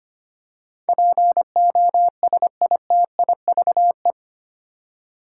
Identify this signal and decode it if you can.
Morse code, heard as a keyed tone, about 25 wpm, 700 Hz. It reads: POSITIVE
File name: Eight Letter Words - 25wpm